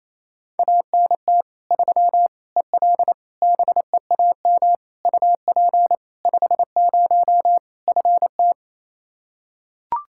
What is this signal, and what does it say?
Morse code, heard as a keyed tone, about 28 wpm, 700 Hz. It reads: ANT 3 EL BEAM UP 50 FT
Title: Practice QSO - Exchange 2 - Info - 28wpm